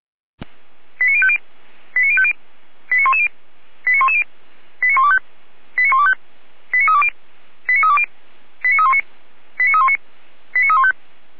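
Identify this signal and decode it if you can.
Five-tone selective call (ZVEI), heard as quick runs of five tones, tones of about 70 ms. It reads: 89050 89050 89110 89110 89125 89125 89320 89320 89319 89319 89315
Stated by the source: ZVEI five-tone selective call received on an SDR.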